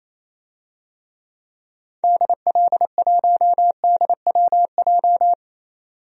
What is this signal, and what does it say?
Morse code, heard as a keyed tone, about 27 wpm, 700 Hz. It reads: DL1DWJ
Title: DXCC-100 Call Signs - 6 Characters - 28wpm